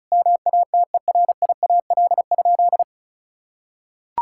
Morse code, heard as a keyed tone, about 34 wpm, 700 Hz. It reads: MATERIAL?
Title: Sentences from Top 1000 Words - Review Part 5 - 35wpm